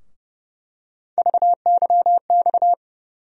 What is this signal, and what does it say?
Morse code, heard as a keyed tone, about 30 wpm, 700 Hz. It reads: VYX
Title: USA Call Sign Postfixes - 3 Letters 30wpm